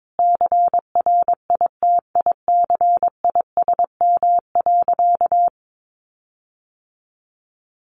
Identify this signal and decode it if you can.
Morse code, heard as a keyed tone, about 22 wpm, 700 Hz. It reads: CRITICISM.
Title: Sentences from Top 1300 Words 22wpm